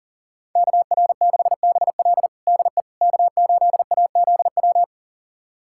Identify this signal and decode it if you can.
Morse code, heard as a keyed tone, about 37 wpm, 700 Hz. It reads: KR6BL DE K8AZW
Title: USA Callsign DE Callsign - 5 Characters - 40wpm